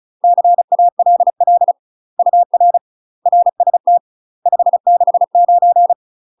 Morse code, heard as a keyed tone, about 36 wpm, 700 Hz. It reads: CALL UR RST 569
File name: UK MASTER QSOs TRAINING / 35WPM / UK MASTER QSOs 0020